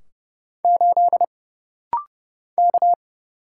Morse code, heard as a keyed tone, about 30 wpm, 700 Hz. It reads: 8 K
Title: Single Letter-Number Alt - Rapid-Fire 30wpm